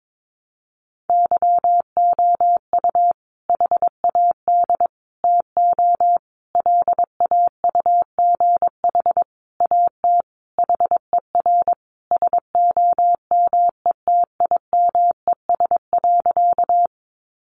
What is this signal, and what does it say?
Morse code, heard as a keyed tone, about 22 wpm, 700 Hz. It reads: YOU HAD TO LAUGH AT HER SOMETIMES.
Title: Sentences from Top 600 Words 22wpm